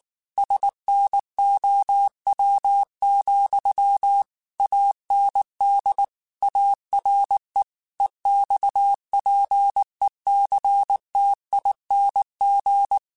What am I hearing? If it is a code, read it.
Morse code, heard as a keyed tone, about 19 wpm, 800 Hz. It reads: SNOW, AND ARE EXPECTING
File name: QSOs / QSOs-mp3s / WA7QOX 1@18